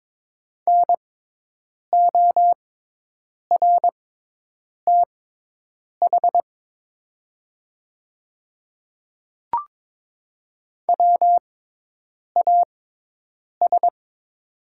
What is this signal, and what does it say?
Morse code, heard as a keyed tone, about 22 wpm, spaced as for 8 wpm, 700 Hz. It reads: NORTH WAS
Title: 6x Character Spacing - Top 300 Words - 22wpm